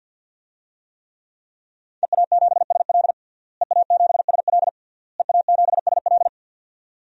Morse code, heard as a keyed tone, about 51 wpm, 700 Hz. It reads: EA7SL EA7SL EA7SL
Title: DXCC-100 Call Signs - 5 Characters - Repeated 3x - 50wpm